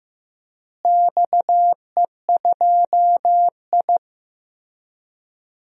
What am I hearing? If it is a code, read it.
Morse code, heard as a keyed tone, about 16 wpm, 700 Hz. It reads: XE2I
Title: DXCC-100 Call Signs - 4 Characters - Repeated 3x - 15wpm